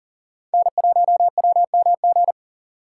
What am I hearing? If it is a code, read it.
Morse code, heard as a keyed tone, about 37 wpm, 700 Hz. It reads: N1WMG
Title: USA Call Signs - 5 Characters - 2x Character Spacing - 40wpm